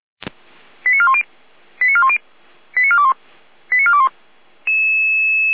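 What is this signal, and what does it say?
Five-tone selective call (ZVEI), heard as quick runs of five tones, tones of about 70 ms. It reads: 89410 89410 89421 89421
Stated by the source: ZVEI five-tone selective call 88084, received with an SDR, served from Feuerwehr Orlamünde's audio server.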